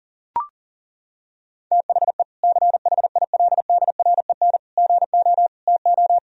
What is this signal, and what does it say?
Morse code, heard as a keyed tone, about 37 wpm, 700 Hz. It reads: THE CHILDREN GO TO
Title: Sentences from Top 1000 Words - Review Part 1 - 40wpm